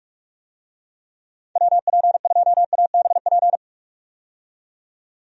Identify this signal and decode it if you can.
Morse code, heard as a keyed tone, about 46 wpm, 700 Hz. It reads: WP2ABP
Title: USA Call Signs - 6 Characters - 45wpm